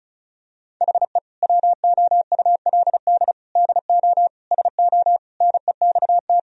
Morse code, heard as a keyed tone, about 36 wpm, 700 Hz. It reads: HE WOULD DO SO NEXT
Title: Sentences from Top 1000 Words - Review Part 1 - 35wpm